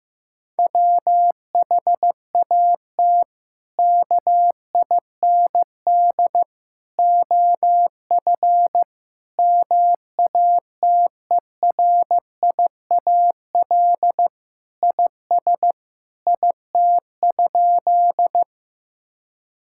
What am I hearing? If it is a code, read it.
Morse code, heard as a keyed tone, about 15 wpm, 700 Hz. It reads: WHAT KIND OF MATERIAL IS IT?
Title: Sentences from Top 1000 Words - Review Part 5 - 15wpm